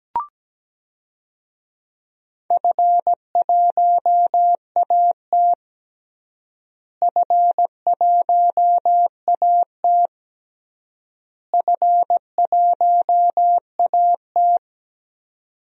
Morse code, heard as a keyed tone, about 17 wpm, 700 Hz. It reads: F1AT F1AT F1AT
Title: DXCC-100 Call Signs - 4 Characters - Repeated 3x - 17wpm